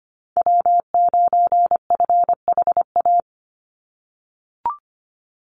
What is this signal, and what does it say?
Morse code, heard as a keyed tone, about 25 wpm, 700 Hz. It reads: W9FHA
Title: NA SOTA Chaser Callsigns 25wpm